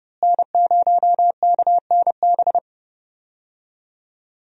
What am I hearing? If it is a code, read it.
Morse code, heard as a keyed tone, about 31 wpm, 700 Hz. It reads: N0KNB
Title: USA Call Signs - 5 Characters - Repeated 2x - 30wpm